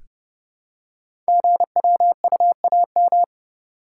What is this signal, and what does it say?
Morse code, heard as a keyed tone, about 30 wpm, 700 Hz. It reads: GWUAM
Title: Code Groups - 5 Letters - 30wpm